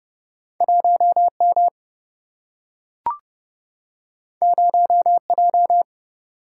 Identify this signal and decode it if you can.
Morse code, heard as a keyed tone, about 30 wpm, 700 Hz. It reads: 1 M 0 J
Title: Single-Letter Single-Number 30wpm